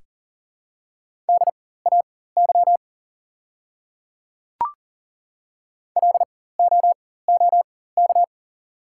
Morse code, heard as a keyed tone, about 40 wpm, spaced as for 19 wpm, 690 Hz. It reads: DAY LOOK
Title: 4x Character Spacing - Top 100 Words - 40wpm